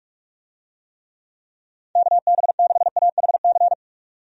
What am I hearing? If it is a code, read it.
Morse code, heard as a keyed tone, about 44 wpm, 700 Hz. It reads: KB6AHC
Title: USA Call Signs - 6 Characters - 45wpm